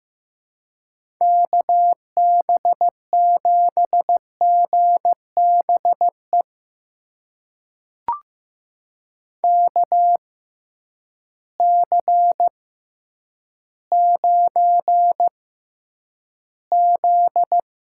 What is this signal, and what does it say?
Morse code, heard as a keyed tone, about 15 wpm, spaced as for 9 wpm, 700 Hz. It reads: KB7GBE KC9Z
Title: USA Call Signs - 6 Characters - 6x Character Spacing - 15wpm